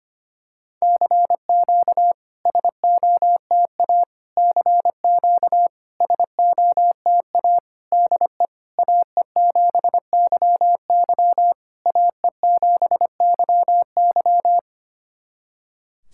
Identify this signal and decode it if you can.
Morse code, heard as a keyed tone, about 25 wpm, 700 Hz. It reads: CQ SOTA CQ SOTA DE AE7YY AE7YY
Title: SOTA QSO Practice 25wpm